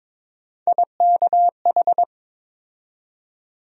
Morse code, heard as a keyed tone, about 22 wpm, 700 Hz. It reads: IKH
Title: USA Call Sign Postfixes - 3 Letters 22wpm